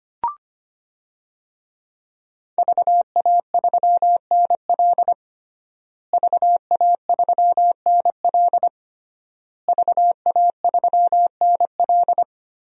Morse code, heard as a keyed tone, about 25 wpm, 700 Hz. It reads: VA3NL VA3NL VA3NL
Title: DXCC-100 Call Signs - 5 Characters - Repeated 3x - 25wpm